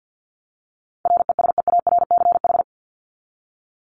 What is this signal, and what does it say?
Morse code, heard as a keyed tone, about 50 wpm, 700 Hz. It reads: RESEARCH
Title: Top 501-750 Words 50wpm